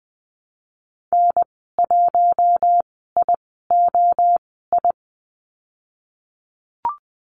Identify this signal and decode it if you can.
Morse code, heard as a keyed tone, about 20 wpm, spaced as for 15 wpm, 700 Hz. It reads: N1IOI
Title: 076 Lesson 13g - Letter L - Call Signs 20wpm